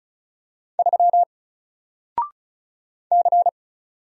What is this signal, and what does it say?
Morse code, heard as a keyed tone, about 36 wpm, 700 Hz. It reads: 3 C
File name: Single-Letter-Number-35wpm